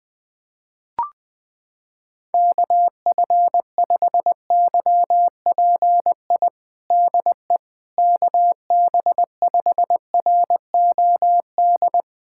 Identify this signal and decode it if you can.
Morse code, heard as a keyed tone, about 20 wpm, 690 Hz. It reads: KF5YPI DE KB5ROD
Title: USA Callsign DE Callsign 20wpm